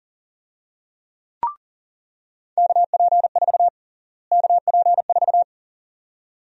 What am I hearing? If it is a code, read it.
Morse code, heard as a keyed tone, about 40 wpm, 700 Hz. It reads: KP4 KP4
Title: USA Call Sign Prefixes - Repeated 2x - 40wpm